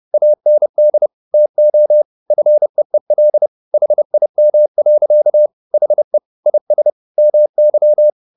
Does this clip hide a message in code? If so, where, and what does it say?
Yes: Morse code from 0.1 to 8.1 s, heard as a keyed tone, about 30 wpm, 600 Hz. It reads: AND TO FEEL HIM. HE IS MY